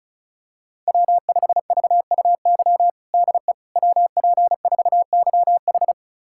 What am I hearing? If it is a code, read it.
Morse code, heard as a keyed tone, about 36 wpm, 690 Hz. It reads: W5VUY DE WP4YH
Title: USA Callsign DE Callsign - 5 Characters - 35wpm